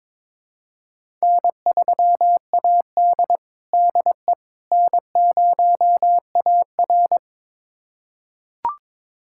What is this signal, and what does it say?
Morse code, heard as a keyed tone, about 22 wpm, 700 Hz. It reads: N3AD DE N0AR
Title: USA Callsign DE Callsign - 4 Characters - 22wpm